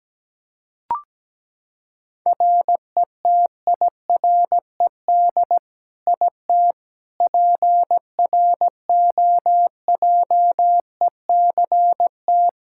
Morse code, heard as a keyed tone, about 17 wpm, 700 Hz. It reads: RETIRED IT PROJECT
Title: Practice QSO - Exchange 3 - More Info - 17wpm